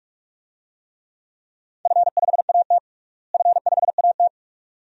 Morse code, heard as a keyed tone, about 44 wpm, 700 Hz. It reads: F5AT F5AT
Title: DXCC-100 Call Signs - 4 Characters - Repeated 2x - 45wpm